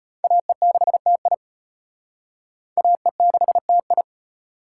Morse code, heard as a keyed tone, about 31 wpm, 700 Hz. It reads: AE6TI AE6TI
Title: Speed Racing - USA 2x2 Call Signs 25wpm